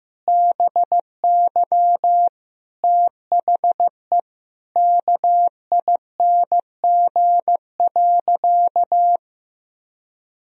Morse code, heard as a keyed tone, about 15 wpm, 700 Hz. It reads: BY THE KING.
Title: Sentences from Top 1000 Words - Review Part 4 - 15wpm